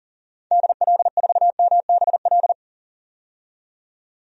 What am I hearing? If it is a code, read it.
Morse code, heard as a keyed tone, about 40 wpm, 700 Hz. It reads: DL4MBL
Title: DXCC-100 Call Signs - 6 Characters - 40wpm